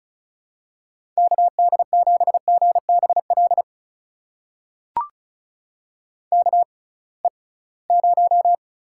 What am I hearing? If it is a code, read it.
Morse code, heard as a keyed tone, about 36 wpm, spaced as for 23 wpm, 700 Hz. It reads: KD7GBL KE0
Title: USA Call Signs - 6 Characters - 6x Character Spacing - 35wpm